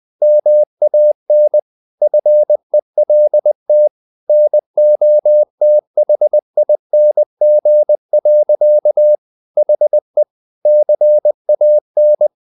Morse code, heard as a keyed tone, about 20 wpm, 600 Hz. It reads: MAN FELT NOTHING. HE CAN
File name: the old man and the sea / the old man and the sea part 1 20wpm